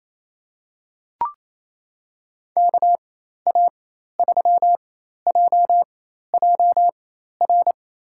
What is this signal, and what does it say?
Morse code, heard as a keyed tone, about 28 wpm, spaced as for 13 wpm, 700 Hz. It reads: KA3JJR
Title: USA Call Signs - 6 Characters - 4x Character Spacing - 28wpm